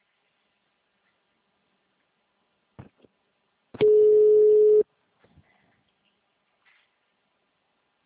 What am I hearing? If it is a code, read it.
Ringback tone (Europe, 425 Hz).